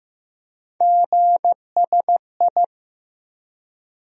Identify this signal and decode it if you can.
Morse code, heard as a keyed tone, about 15 wpm, 700 Hz. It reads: GSI
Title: USA Call Sign Postfixes - 3 Letters 15wpm